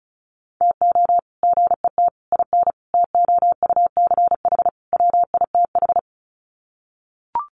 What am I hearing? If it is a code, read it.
Morse code, heard as a keyed tone, about 34 wpm, 700 Hz. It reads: TO GET IN TOUCH WITH
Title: Sets of 5 Words Encore2 35wpm